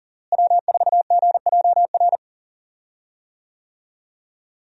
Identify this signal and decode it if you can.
Morse code, heard as a keyed tone, about 40 wpm, 700 Hz. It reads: W4GJR
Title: USA Call Signs - 5 Characters - Repeated 3x - 40wpm